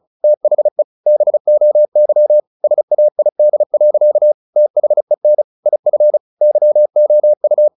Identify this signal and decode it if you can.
Morse code, heard as a keyed tone, about 34 wpm, 600 Hz. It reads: THE BOY SAID. THEN IF YOU
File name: the old man and the sea / the old man and the sea 1 35wpm